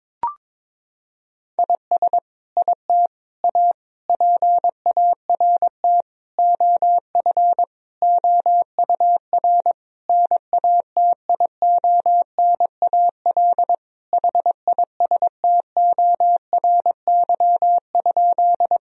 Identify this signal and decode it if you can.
Morse code, heard as a keyed tone, about 22 wpm, 700 Hz. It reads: IS IT A PART OF OUR NATIONAL HISTORY?
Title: Sentences from Top 1000 Words - Review Part 4 - 22wpm